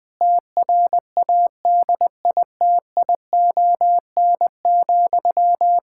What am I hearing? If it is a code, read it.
Morse code, heard as a keyed tone, about 20 wpm, 700 Hz. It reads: TRADITION,
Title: Pangram Challenge v2 - 20wpm